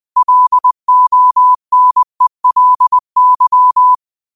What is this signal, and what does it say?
Morse code, heard as a keyed tone, about 20 wpm, 1000 Hz. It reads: LONELY
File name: qotdfiles / 20wpm